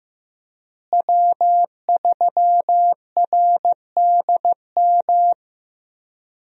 Morse code, heard as a keyed tone, about 15 wpm, 700 Hz. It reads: W3RDM
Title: USA Call Signs - 5 Characters - Repeated 2x - 15wpm